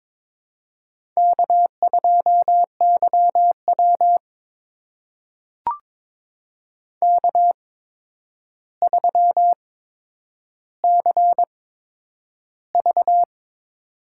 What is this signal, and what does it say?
Morse code, heard as a keyed tone, about 22 wpm, spaced as for 9 wpm, 700 Hz. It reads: K2YW K3CV
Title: USA Call Signs - 4 Characters - 8x Character Spacing - 22wpm